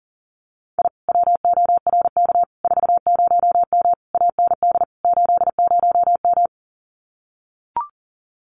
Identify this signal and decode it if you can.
Morse code, heard as a keyed tone, about 40 wpm, 700 Hz. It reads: I WORK 40M AND 80M
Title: Common QSO Phrases Encore 40wpm